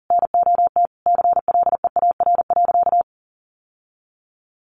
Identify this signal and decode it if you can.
Morse code, heard as a keyed tone, about 40 wpm, 700 Hz. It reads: NOT CLEAR.
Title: Sentences from Top 1300 Words 40wpm